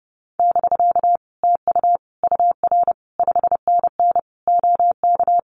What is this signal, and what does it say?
Morse code, heard as a keyed tone, about 30 wpm, 700 Hz. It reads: <BK> TU UR 5NN OK
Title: Common QSO Phrases Encore 30wpm